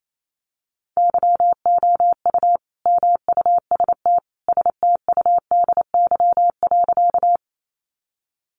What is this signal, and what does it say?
Morse code, heard as a keyed tone, about 28 wpm, 700 Hz. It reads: YOU MUST STUDY.
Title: Sentences from Top 300 Words 28wpm